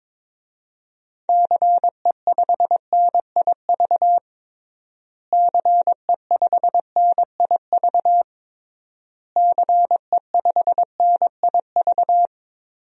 Morse code, heard as a keyed tone, about 22 wpm, 700 Hz. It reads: CE5NIV CE5NIV CE5NIV
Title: DXCC-100 Call Signs - 6 Characters - Repeated 3x - 22wpm